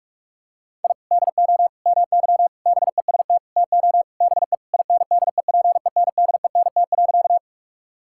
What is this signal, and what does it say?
Morse code, heard as a keyed tone, about 46 wpm, 700 Hz. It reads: I DO MY BEST TO BE INDEPENDENT.
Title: Sentences from Top 1000 Words - Review Part 6 - 45wpm